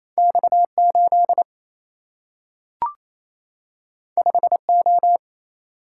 Morse code, heard as a keyed tone, about 28 wpm, 700 Hz. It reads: X 8 5 O
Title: Single-Letter Single-Number 28wpm